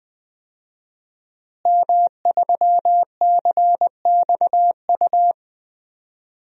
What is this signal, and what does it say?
Morse code, heard as a keyed tone, about 20 wpm, 700 Hz. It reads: M3CXU
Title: DXCC-100 Call Signs - 5 Characters - Repeated 3x - 20wpm